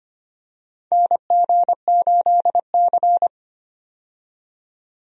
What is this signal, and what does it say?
Morse code, heard as a keyed tone, about 25 wpm, 700 Hz. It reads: NG8C
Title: USA Call Signs - 4 Characters - 25wpm